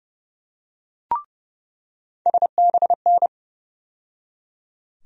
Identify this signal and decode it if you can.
Morse code, heard as a keyed tone, about 32 wpm, 700 Hz. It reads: SBN
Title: USA Call Sign Postfixes - 3 Letters 30wpm